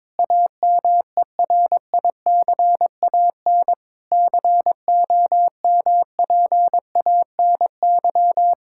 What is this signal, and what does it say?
Morse code, heard as a keyed tone, about 22 wpm, 700 Hz. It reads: AMERICAN COMPANY
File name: practice / 22wpm / Sentences with 3-Word Phrases - 22wpm